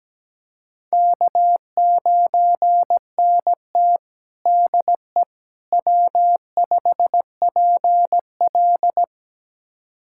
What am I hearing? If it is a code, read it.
Morse code, heard as a keyed tone, about 17 wpm, 700 Hz. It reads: K9NT DE W5PL
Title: USA Callsign DE Callsign - 4 Characters - 17wpm